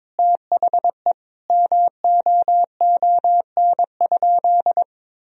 Morse code, heard as a keyed tone, about 22 wpm, 700 Hz. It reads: THE MOON?
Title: Sentences from Top 1000 Words - Review Part 3 - 22wpm